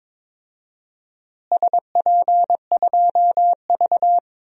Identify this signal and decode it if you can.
Morse code, heard as a keyed tone, about 21 wpm, 700 Hz. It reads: SP2V
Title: DXCC-100 Call Signs - 4 Characters - Repeated 2x - 22wpm